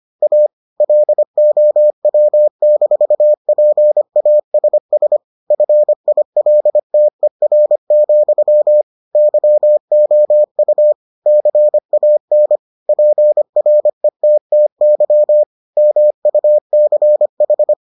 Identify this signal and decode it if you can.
Morse code, heard as a keyed tone, about 25 wpm, 600 Hz. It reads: A LOW-PASS FILTER, YOU CAN PRETTY MUCH